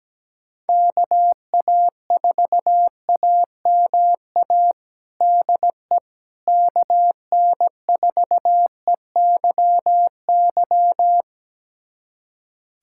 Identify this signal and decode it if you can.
Morse code, heard as a keyed tone, about 17 wpm, 700 Hz. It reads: KA4AMA DE KN4EYY
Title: USA Callsign DE Callsign 17wpm